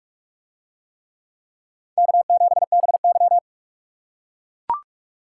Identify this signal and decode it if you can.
Morse code, heard as a keyed tone, about 44 wpm, 700 Hz. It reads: K7BY